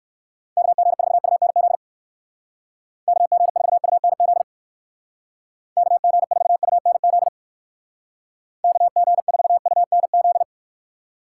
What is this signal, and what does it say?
Morse code, heard as a keyed tone, about 56 wpm, 700 Hz. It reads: KG4UNZ KG4UNZ KG4UNZ KG4UNZ
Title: Speed Racing - USA Call Signs 45wpm